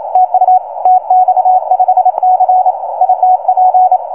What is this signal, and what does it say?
Morse code, heard as a keyed tone, about 28 wpm, 750 Hz. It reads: TU TX5C ITP